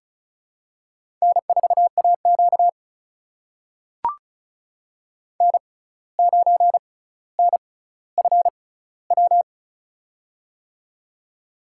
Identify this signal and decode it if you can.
Morse code, heard as a keyed tone, about 36 wpm, spaced as for 17 wpm, 700 Hz. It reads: N4AQ N9NFW
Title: USA Call Signs - 6x Character Spacing - 35wpm